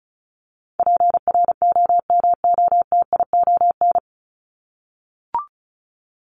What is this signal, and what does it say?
Morse code, heard as a keyed tone, about 34 wpm, 700 Hz. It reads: PROMOTION
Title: Top 1501-1750 Words 35wpm